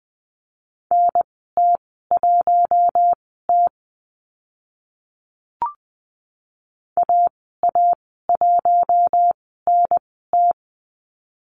Morse code, heard as a keyed tone, about 20 wpm, spaced as for 15 wpm, 700 Hz. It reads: NT1T AA1NT
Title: 042 Lesson 8g - Number 1 - Call Signs 20wpm